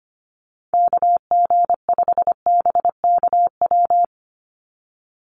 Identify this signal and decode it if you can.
Morse code, heard as a keyed tone, about 25 wpm, 700 Hz. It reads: KG5BKW
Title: USA Call Signs 25wpm